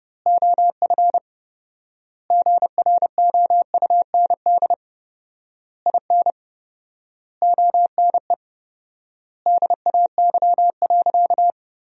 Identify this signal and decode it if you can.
Morse code, heard as a keyed tone, about 30 wpm, 700 Hz. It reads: OF GROUND IN ONE DAY.
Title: 4x Word-Spacing - Sentences from Top 500 Words - 30wpm